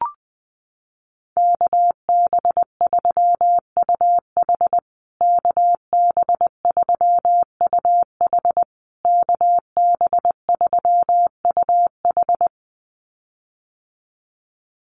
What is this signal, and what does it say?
Morse code, heard as a keyed tone, about 20 wpm, 700 Hz. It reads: KB3UH KB3UH KB3UH
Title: USA Call Signs Repeated 3x 20wpm